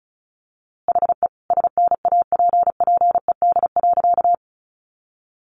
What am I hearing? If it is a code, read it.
Morse code, heard as a keyed tone, about 34 wpm, 700 Hz. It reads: HE SNAPPED.
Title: A Sound Of Thunder 35wpm